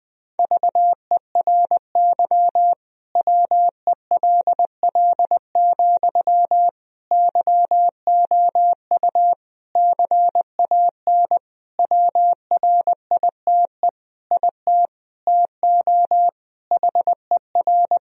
Morse code, heard as a keyed tone, about 20 wpm, 700 Hz. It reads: VERY WELL, YOU CAN WRITE IT TO HER
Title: Sentences from Top 1000 Words - Review Part 4 - 20wpm